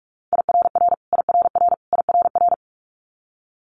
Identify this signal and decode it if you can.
Morse code, heard as a keyed tone, about 46 wpm, 700 Hz. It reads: IRR IRR IRR
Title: Multi-letter Phonemes Repeated 3x 45wpm